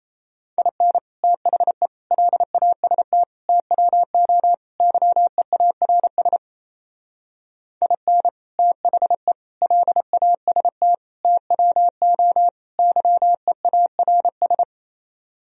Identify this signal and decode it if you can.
Morse code, heard as a keyed tone, about 35 wpm, 700 Hz. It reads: IN THE LAST TWO YEARS IN THE LAST TWO YEARS